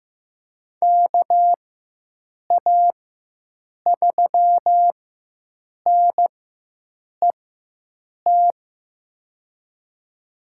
Morse code, heard as a keyed tone, about 15 wpm, spaced as for 7 wpm, 700 Hz. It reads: KA3NET
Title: USA Call Signs - 6 Characters - 4x Character Spacing - 15wpm